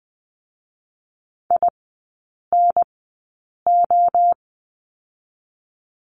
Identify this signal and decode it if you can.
Morse code, heard as a keyed tone, about 20 wpm, spaced as for 8 wpm, 700 Hz. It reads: INO
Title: 041 Lesson 8e - Number 1 - Three Characters 20wpm